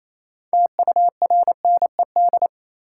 Morse code, heard as a keyed tone, about 28 wpm, 700 Hz. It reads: TURNED
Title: Sets of 4 Words - New Word Drill - 28wpm